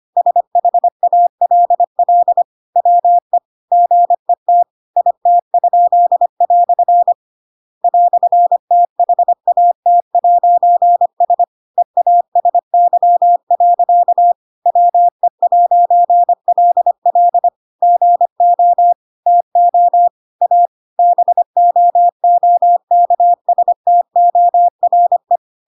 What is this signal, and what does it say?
Morse code, heard as a keyed tone, about 25 wpm, 700 Hz. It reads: SHALL WE GET IT?" "THAT'S EASY. WE'LL GO TO A BOOKSTORE
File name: Pinocchio / Pinocchio ch8 25 25 ik1hge